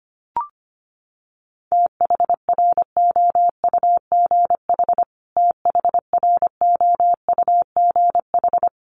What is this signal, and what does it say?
Morse code, heard as a keyed tone, about 25 wpm, 700 Hz. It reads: THROUGH THROUGH
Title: Top 200 Words Repeated 25wpm